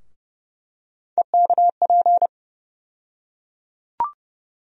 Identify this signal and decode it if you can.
Morse code, heard as a keyed tone, about 30 wpm, 700 Hz. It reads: EKP
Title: USA Call Sign Postfixes - 3 Letters 30wpm